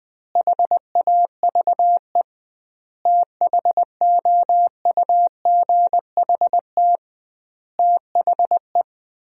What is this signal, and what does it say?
Morse code, heard as a keyed tone, about 20 wpm, 700 Hz. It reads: HAVE THOUGHT THE
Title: 2x Word-Spacing - Sentences from Top 300 Words - 20wpm